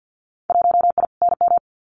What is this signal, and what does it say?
Morse code, heard as a keyed tone, about 50 wpm, 700 Hz. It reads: 1I NM
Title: Winter Field Day Exchanges Encore 50wpm